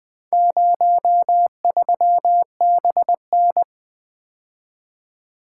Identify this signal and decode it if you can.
Morse code, heard as a keyed tone, about 20 wpm, 700 Hz. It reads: 03BN
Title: Four Letter-Number Combinations - 20wpm